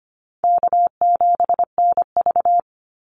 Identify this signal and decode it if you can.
Morse code, heard as a keyed tone, about 25 wpm, 700 Hz. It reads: K7NV
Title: Easy Call Signs Repeated 2x - 25wpm